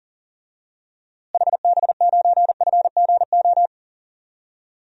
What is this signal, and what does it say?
Morse code, heard as a keyed tone, about 40 wpm, 700 Hz. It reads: HB9FGO
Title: DXCC-100 Call Signs - 6 Characters - Repeated 2x - 40wpm